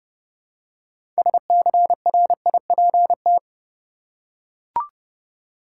Morse code, heard as a keyed tone, about 30 wpm, 700 Hz. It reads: SCRIPT
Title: Six Letter Words - 30wpm